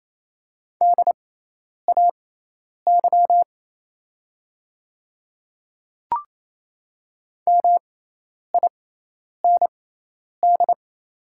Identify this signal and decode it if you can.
Morse code, heard as a keyed tone, about 28 wpm, spaced as for 10 wpm, 700 Hz. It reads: DAY MIND
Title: 6x Character Spacing - Top 500 Words - 28wpm